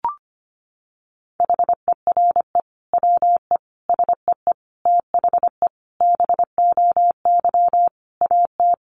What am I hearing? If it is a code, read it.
Morse code, heard as a keyed tone, about 25 wpm, 700 Hz. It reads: HERE WE SEE THE BOY AT
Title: Sentences from Top 300 Words 25wpm